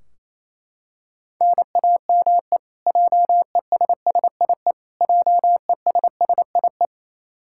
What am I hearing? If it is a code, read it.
Morse code, heard as a keyed tone, about 28 wpm, 700 Hz. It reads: NAME JESSIE JESSIE
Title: Practice QSO - Exchange 1 - Beginning - 28wpm